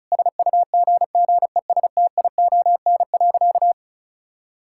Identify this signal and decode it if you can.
Morse code, heard as a keyed tone, about 34 wpm, 700 Hz. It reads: SUGGESTION.
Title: Sentences from Top 1600 Words 35wpm